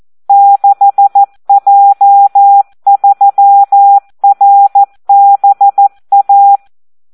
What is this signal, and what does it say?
Morse code, heard as a keyed tone, about 14 wpm, 800 Hz. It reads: 6J3RBA
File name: Audio practica CW / SD 1 14